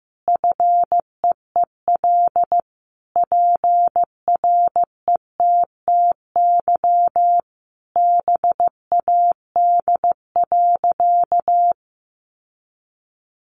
Morse code, heard as a keyed tone, about 15 wpm, 700 Hz. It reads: FEEL PRETTY BAD.
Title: Sentences from Top 1000 Words 15wpm